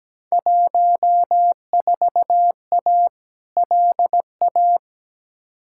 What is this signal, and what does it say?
Morse code, heard as a keyed tone, about 17 wpm, 700 Hz. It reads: 14A LA
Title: Field Day Exchanges Encore - 17wpm